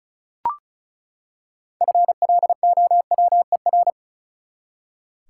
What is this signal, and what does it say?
Morse code, heard as a keyed tone, about 36 wpm, 700 Hz. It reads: FLOWER
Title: Six Letter Words - 35wpm